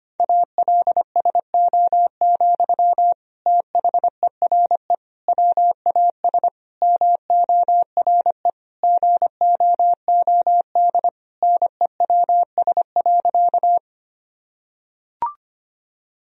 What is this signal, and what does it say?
Morse code, heard as a keyed tone, about 25 wpm, 700 Hz. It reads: ALSO, THERE WAS MORE GOOD NEWS.
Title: Sentences from Top 1000 Words - Review Part 1 - 25wpm